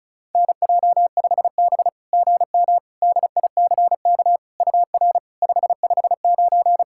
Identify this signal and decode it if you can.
Morse code, heard as a keyed tone, about 36 wpm, 690 Hz. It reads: NJ5B GM DICK UR 559
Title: SOTA QSO Practice 35wpm